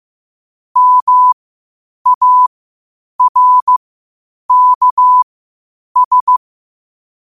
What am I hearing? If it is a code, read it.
Morse code, heard as a keyed tone, about 15 wpm, spaced as for 8 wpm, 1000 Hz. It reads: MARKS
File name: qotdfiles / 8wpm